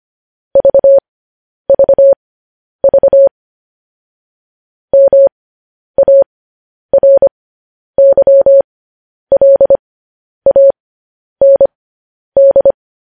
Morse code, heard as a keyed tone, about 25 wpm, spaced as for 10 wpm, 550 Hz. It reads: VVV MARYLAND
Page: news headlines in morse code at 10 wpm